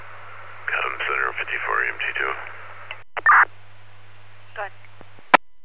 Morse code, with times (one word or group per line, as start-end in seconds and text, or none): none